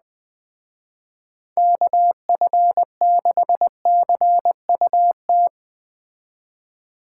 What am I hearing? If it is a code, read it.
Morse code, heard as a keyed tone, about 20 wpm, 700 Hz. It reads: KF6CUT
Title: Speed Racing - USA 2x3 Call Signs 17wpm